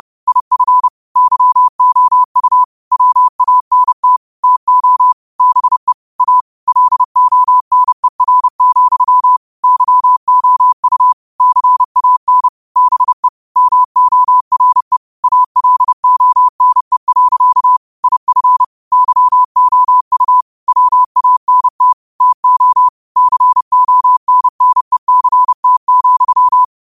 Morse code, heard as a keyed tone, about 30 wpm, 1000 Hz. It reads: IF YOU WANT TO BE A LONER, YOU CAN BE MORE ALONE. IF YOU WANT TO CONNECT,
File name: qotdfiles / 30wpm